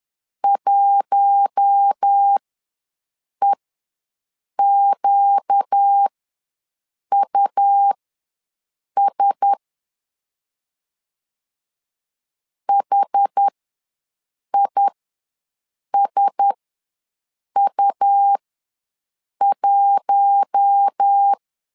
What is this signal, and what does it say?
Morse code, heard as a keyed tone, about 10 wpm, spaced as for 6 wpm, 800 Hz. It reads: 1EQUS HISU1
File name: LEZIONI / 4 Q1S ws 10wpm 2208